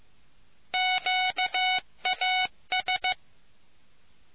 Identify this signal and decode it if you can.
Morse code, heard as a keyed tone, about 15 wpm, 740 Hz. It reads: QAS